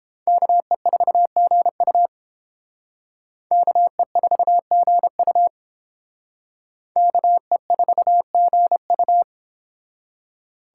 Morse code, heard as a keyed tone, about 30 wpm, 700 Hz. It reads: KE4GU KE4GU KE4GU